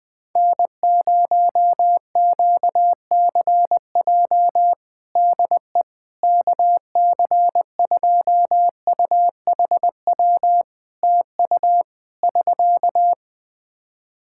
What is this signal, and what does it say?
Morse code, heard as a keyed tone, about 20 wpm, 700 Hz. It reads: N0QCJ DE KC2UHW TU <SK>
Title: K1USN SST QSO Practice 20wpm